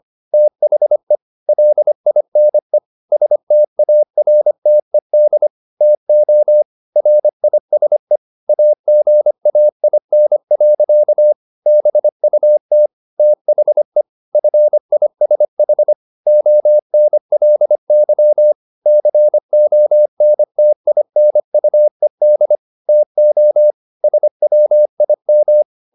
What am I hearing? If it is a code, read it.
Morse code, heard as a keyed tone, about 25 wpm, 600 Hz. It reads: THE LINE STARTED TO RISE AGAIN. BUT THE FISH ONLY CONTINUED TO SWIM
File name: the old man and the sea part 1 25wpm